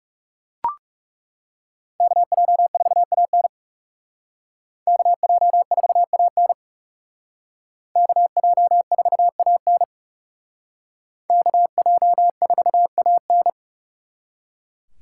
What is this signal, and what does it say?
Morse code, heard as a keyed tone, about 36 wpm, 700 Hz. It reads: KJ4AN KJ4AN KJ4AN KJ4AN
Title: Speed Racing - USA 2x2 Call Signs 30wpm